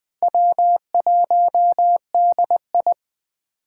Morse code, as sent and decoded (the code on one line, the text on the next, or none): .-- .---- -.. ..
W1DI